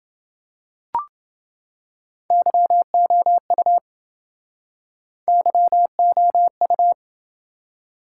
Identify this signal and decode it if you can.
Morse code, heard as a keyed tone, about 28 wpm, 700 Hz. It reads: YOU YOU